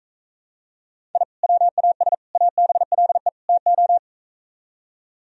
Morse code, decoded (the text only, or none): I WAS ABLE TO